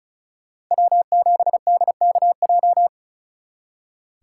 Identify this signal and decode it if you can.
Morse code, heard as a keyed tone, about 43 wpm, 700 Hz. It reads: W7DKJ
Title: Speed Racing - USA 1x3 Call Signs 35wpm